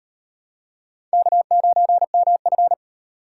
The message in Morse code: K9MF

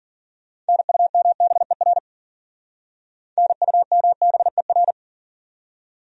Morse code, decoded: NUMBER NUMBER

